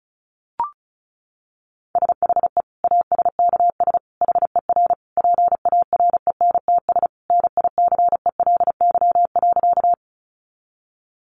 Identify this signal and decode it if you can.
Morse code, heard as a keyed tone, about 34 wpm, 700 Hz. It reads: SHE ASKS HER PARENTS NICELY.
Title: Easy Phrases 35wpm